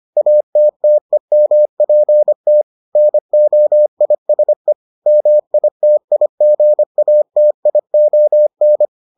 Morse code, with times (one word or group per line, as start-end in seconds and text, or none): 0.2-2.6 ATTEMPT
3.0-4.7 NOISE
5.1-8.9 MITIGATION